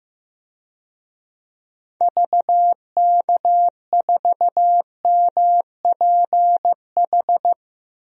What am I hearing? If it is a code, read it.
Morse code, heard as a keyed tone, about 15 wpm, 700 Hz. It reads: VK4MPH
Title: DXCC-100 Call Signs - 6 Characters - Repeated 3x - 15wpm